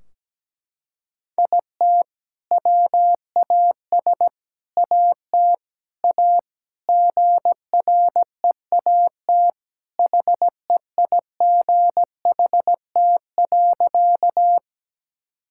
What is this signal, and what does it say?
Morse code, heard as a keyed tone, about 17 wpm, 700 Hz. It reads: IT WAS AT A GREAT HEIGHT.
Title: Sentences from Top 1600 Words 17wpm